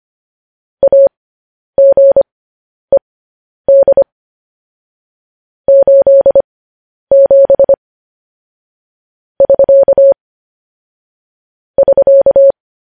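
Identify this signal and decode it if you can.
Morse code, heard as a keyed tone, about 25 wpm, spaced as for 10 wpm, 550 Hz. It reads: AGED 87 <SK> <SK>
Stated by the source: Morse code news headlines broadcast, opening with vvv.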